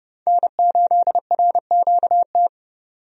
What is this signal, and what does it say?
Morse code, heard as a keyed tone, about 31 wpm, 700 Hz. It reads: N8RQT
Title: USA Call Signs - 5 Characters - 30wpm